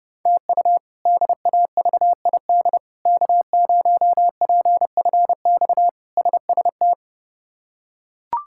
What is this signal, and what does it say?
Morse code, heard as a keyed tone, about 30 wpm, 700 Hz. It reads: TU DAVID K0PFX SST
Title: K1USN SST QSO Practice - 30wpm